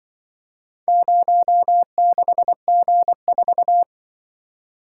Morse code, heard as a keyed tone, about 25 wpm, 700 Hz. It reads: 06G4
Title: Speed-Racing - Four Letter-Number Combinations - 20wpm